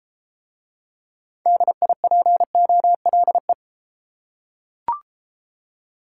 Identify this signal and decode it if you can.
Morse code, heard as a keyed tone, about 31 wpm, 700 Hz. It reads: DIPOLE